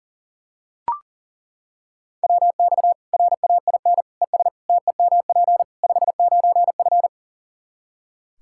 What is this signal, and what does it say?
Morse code, heard as a keyed tone, about 40 wpm, 700 Hz. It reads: WX RAIN ES TEMP 59F
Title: Practice QSO - Exchange 2 - Info - 40wpm